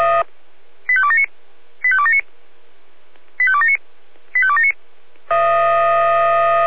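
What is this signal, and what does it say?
Five-tone selective call (ZVEI), heard as quick runs of five tones, tones of about 70 ms. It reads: 85279 85279 85279 85279
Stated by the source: ZVEI five-tone selective call received by SDR.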